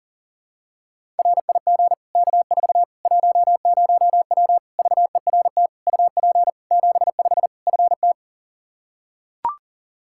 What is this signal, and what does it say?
Morse code, heard as a keyed tone, about 40 wpm, 700 Hz. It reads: RIG K4 10W VERT UP 75 FT
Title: Practice QSO - Exchange 2 - Info - 40wpm